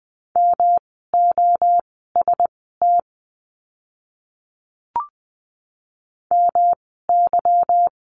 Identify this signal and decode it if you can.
Morse code, heard as a keyed tone, about 20 wpm, spaced as for 15 wpm, 700 Hz. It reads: MOST MY
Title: 230 Lesson 37f - Letter x - Words 20wpm